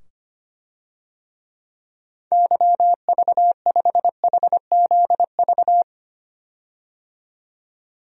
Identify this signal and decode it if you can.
Morse code, heard as a keyed tone, about 25 wpm, 700 Hz. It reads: YV5HZV
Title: DXCC-100 Call Signs - 6 Characters - 25wpm